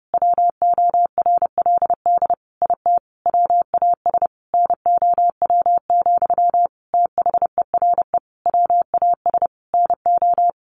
Morse code, heard as a keyed tone, about 30 wpm, 700 Hz. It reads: WORLD IT WAS NOW, THERE WAS NO
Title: A Sound Of Thunder 30wpm